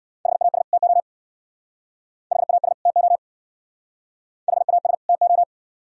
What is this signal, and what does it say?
Morse code, heard as a keyed tone, about 66 wpm, 700 Hz. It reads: HAS TO HAS TO HAS TO